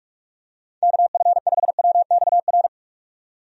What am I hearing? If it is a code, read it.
Morse code, heard as a keyed tone, about 46 wpm, 700 Hz. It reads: KF5WXR
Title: USA Call Signs - 6 Characters - 45wpm